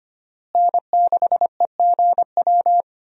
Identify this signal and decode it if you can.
Morse code, heard as a keyed tone, about 25 wpm, 700 Hz. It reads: N6EGW
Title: USA Call Signs - 5 Characters - Repeated 3x - 25wpm